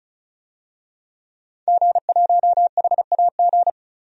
Morse code, heard as a keyed tone, about 36 wpm, 700 Hz. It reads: G1HAG
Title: DXCC-100 Call Signs - 5 Characters - 35wpm